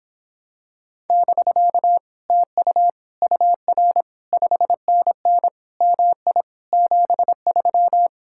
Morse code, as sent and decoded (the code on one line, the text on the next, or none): -...-.- / - ..- / ..- .-. / ..... -. -. / -- .. / --... ...--
<BK> TU UR 5NN MI 73